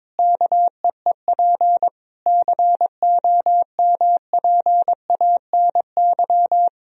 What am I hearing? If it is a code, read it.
Morse code, heard as a keyed tone, about 22 wpm, 700 Hz. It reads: KEEP COMPANY
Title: Sentences from Top 1000 Words - Review Part 3 - 22wpm